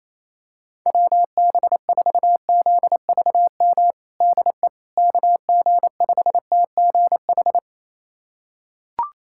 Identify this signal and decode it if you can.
Morse code, heard as a keyed tone, about 28 wpm, 700 Hz. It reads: WB4ZVM DE KG5TGH
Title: USA Callsign DE Callsign - 6 Characters - 28wpm